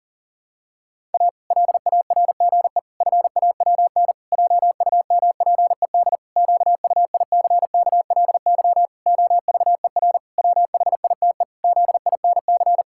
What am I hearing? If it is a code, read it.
Morse code, heard as a keyed tone, about 40 wpm, 700 Hz. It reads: A LARGE FAWN JUMPED QUICKLY OVER WHITE ZINC
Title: Pangram Challenge v2 - 40wpm